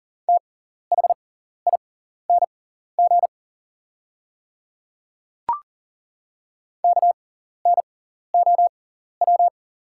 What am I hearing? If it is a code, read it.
Morse code, heard as a keyed tone, about 39 wpm, spaced as for 14 wpm, 690 Hz. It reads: THING KNOW
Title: 6x Character Spacing - Top 100 Words - 40wpm